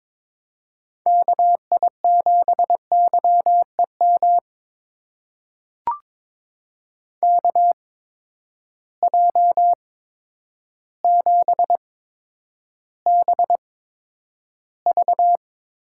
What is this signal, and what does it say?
Morse code, heard as a keyed tone, about 22 wpm, spaced as for 10 wpm, 700 Hz. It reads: KI7YEM KJ7BV
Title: USA Call Signs - 6 Characters - 8x Character Spacing - 22wpm